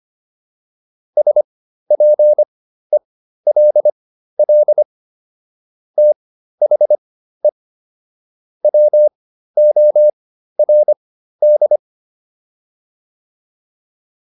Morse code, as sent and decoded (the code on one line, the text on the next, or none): ... .--. .-.. .-.. / - .... / .-- --- .-. -..
SPELL THE WORD